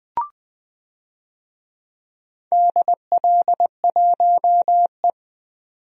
Morse code, heard as a keyed tone, about 20 wpm, 700 Hz. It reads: DL1E
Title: DXCC-100 Call Signs - 4 Characters - 20wpm